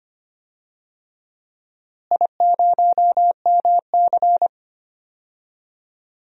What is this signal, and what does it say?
Morse code, heard as a keyed tone, about 25 wpm, 700 Hz. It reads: I0MC
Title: DXCC-100 Call Signs - 4 Characters - Repeated 3x - 25wpm